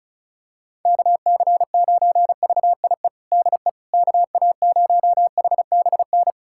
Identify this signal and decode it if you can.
Morse code, heard as a keyed tone, about 34 wpm, 700 Hz. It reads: KC9VIE DE KA0HBN
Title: USA Callsign DE Callsign - 6 Characters - 35wpm